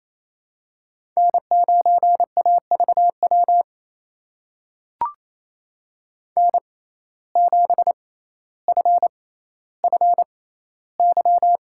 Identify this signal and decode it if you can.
Morse code, heard as a keyed tone, about 28 wpm, spaced as for 14 wpm, 700 Hz. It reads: N9AVW N7FFY
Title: USA Call Signs - 5 Characters - 6x Character Spacing - 28wpm